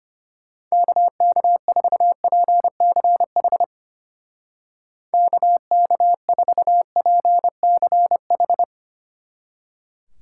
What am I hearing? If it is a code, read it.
Morse code, heard as a keyed tone, about 32 wpm, 700 Hz. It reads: KK4PCH KK4PCH